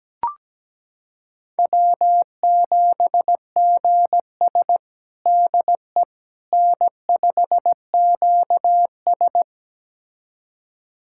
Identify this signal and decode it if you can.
Morse code, heard as a keyed tone, about 17 wpm, 700 Hz. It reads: W7GS DE N5QS
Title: USA Callsign DE Callsign - 4 Characters - 17wpm